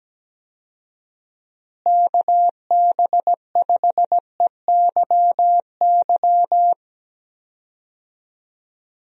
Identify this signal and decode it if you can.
Morse code, heard as a keyed tone, about 17 wpm, 700 Hz. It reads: KB5EYY
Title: USA Call Signs - 6 Characters - Repeated 2x - 17wpm